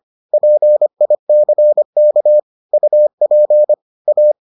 Morse code, heard as a keyed tone, about 25 wpm, 600 Hz. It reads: PICK UP A